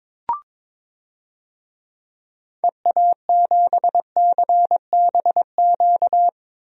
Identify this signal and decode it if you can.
Morse code, heard as a keyed tone, about 22 wpm, 700 Hz. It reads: EA7CBQ
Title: DXCC-100 Call Signs - 6 Characters - 22wpm